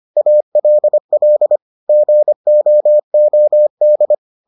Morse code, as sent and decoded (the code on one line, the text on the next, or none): .- .-.. .-.. / --. --- --- -..
ALL GOOD